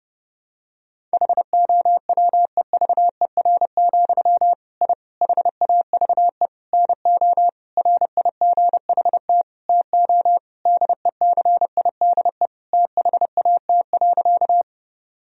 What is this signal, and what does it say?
Morse code, heard as a keyed tone, about 30 wpm, 700 Hz. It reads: HOWEVER, I HAVE NO RIGHT TO DECIDE THAT.
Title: Sentences from Top 1000 Words - Review Part 3 - 30wpm